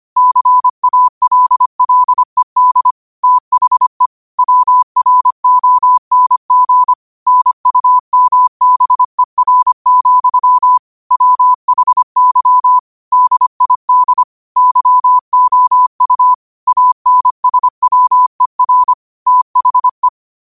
Morse code, heard as a keyed tone, about 25 wpm, 1000 Hz. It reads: CALLED THE WRONG NUMBER, WHY DID YOU ANSWER THE
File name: 25wpm